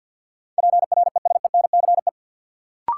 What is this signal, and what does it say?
Morse code, heard as a keyed tone, about 53 wpm, 700 Hz. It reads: PRESENCE
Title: Sets of 3 Words - New Word Drill - 50wpm